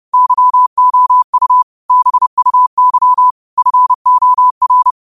Morse code, heard as a keyed tone, about 31 wpm, 1000 Hz. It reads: YOU BUY FOR
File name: qotdfiles / 30wpm